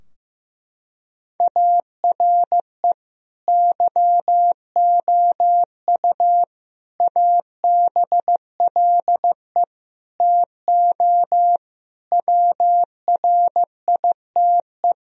Morse code, heard as a keyed tone, about 15 wpm, 700 Hz. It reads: ARE YOU ABLE TO WRITE
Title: Sentences from Top 1000 Words - Review Part 3 - 15wpm